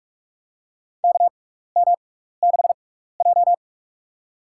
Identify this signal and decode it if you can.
Morse code, heard as a keyed tone, about 46 wpm, spaced as for 15 wpm, 700 Hz. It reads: KM6J
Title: USA Call Signs - 4 Characters - 6x Character Spacing - 45wpm